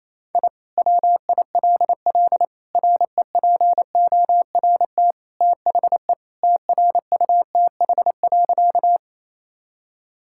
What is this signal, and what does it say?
Morse code, heard as a keyed tone, about 28 wpm, 700 Hz. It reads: I WILL REPORT THE TRUTH.
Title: Sentences from Top 1000 Words - Review Part 6 - 28wpm